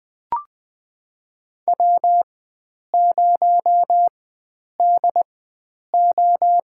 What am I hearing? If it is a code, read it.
Morse code, heard as a keyed tone, about 20 wpm, spaced as for 9 wpm, 700 Hz. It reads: W0DO
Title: USA Call Signs - 4 Characters - 4x Character Spacing - 20wpm